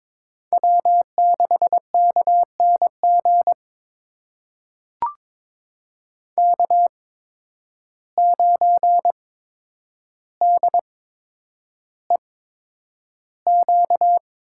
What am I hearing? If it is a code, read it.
Morse code, heard as a keyed tone, about 22 wpm, spaced as for 9 wpm, 700 Hz. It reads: W6KNG K9DEQ
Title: USA Call Signs - 5 Characters - 8x Character Spacing - 22wpm